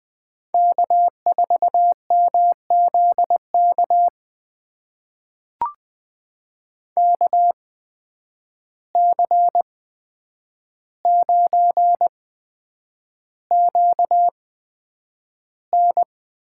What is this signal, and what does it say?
Morse code, heard as a keyed tone, about 20 wpm, spaced as for 9 wpm, 700 Hz. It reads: K4MZK KC9QN
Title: USA Call Signs - 8x Character Spacing - 20wpm